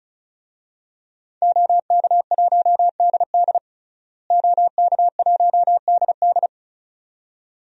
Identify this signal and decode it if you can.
Morse code, heard as a keyed tone, about 36 wpm, 700 Hz. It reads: OK1DD OK1DD
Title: DXCC-100 Call Signs - 5 Characters - Repeated 2x - 35wpm